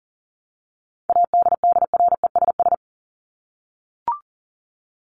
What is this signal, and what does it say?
Morse code, heard as a keyed tone, about 40 wpm, 700 Hz. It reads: ADDRESS
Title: Top 751-1000 Words 40wpm